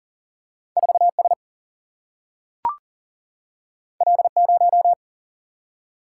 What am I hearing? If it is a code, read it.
Morse code, heard as a keyed tone, about 40 wpm, 700 Hz. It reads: 4 S L 0
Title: Single-Letter Single-Number 40wpm